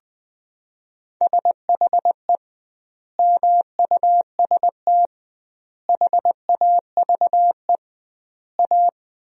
Morse code, heard as a keyed tone, about 20 wpm, 700 Hz. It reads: SHE MUST HAVE A
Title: 2x Word-Spacing - Sentences from Top 300 Words - 20wpm